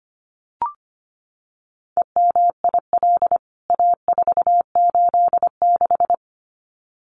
Morse code, heard as a keyed tone, about 25 wpm, 700 Hz. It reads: EMIL A486
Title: CWT Exchange 25wpm